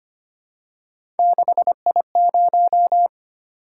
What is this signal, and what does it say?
Morse code, heard as a keyed tone, about 25 wpm, 700 Hz. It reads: 6I0
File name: Three Letter-Number Combinations - 25wpm